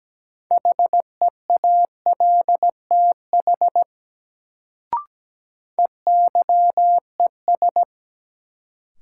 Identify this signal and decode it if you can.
Morse code, heard as a keyed tone, about 17 wpm, 700 Hz. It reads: HEALTH EYES
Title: Sets of 5 Words - New Word Drill - 17wpm